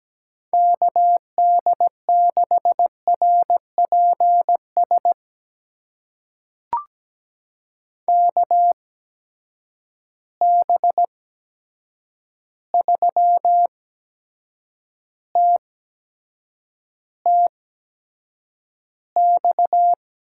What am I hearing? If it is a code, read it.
Morse code, heard as a keyed tone, about 17 wpm, spaced as for 7 wpm, 700 Hz. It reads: KD6RPS KB3TTX
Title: USA Call Signs - 6 Characters - 8x Character Spacing - 17wpm